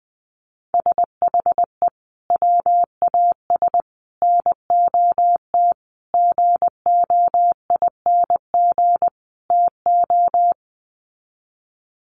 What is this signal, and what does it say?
Morse code, heard as a keyed tone, about 19 wpm, 700 Hz. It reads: SHE WAS NOT GOING TO
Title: Sets of 5 Words Encore2 20wpm